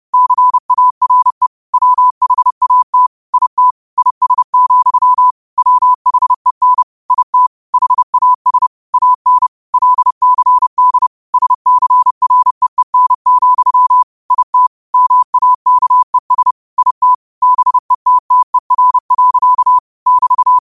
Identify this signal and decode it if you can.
Morse code, heard as a keyed tone, about 30 wpm, 1000 Hz. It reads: CARE WHAT IT IS, WHEN IT HAS AN LCD SCREEN, IT MAKES IT BETTER. <BT>
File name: qotdfiles / 30wpm